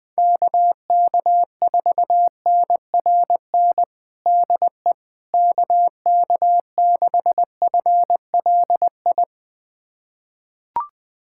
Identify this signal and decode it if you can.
Morse code, heard as a keyed tone, about 20 wpm, 700 Hz. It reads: KK4NRN DE KK6FLI
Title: USA Callsign DE Callsign - 6 Characters - 20wpm